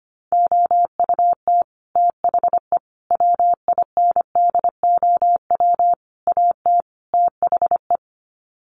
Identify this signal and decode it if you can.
Morse code, heard as a keyed tone, about 25 wpm, 700 Hz. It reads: OUT THE WINDOW AT THE
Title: Sets of 5 Words Encore2 25wpm